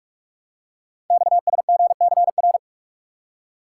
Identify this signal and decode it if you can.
Morse code, heard as a keyed tone, about 44 wpm, 700 Hz. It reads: XSGCR
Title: Code Groups - 5 Letters - 45wpm